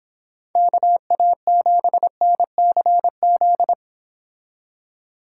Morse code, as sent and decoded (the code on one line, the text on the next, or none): -.- .- --... -. -.-. --..
KA7NCZ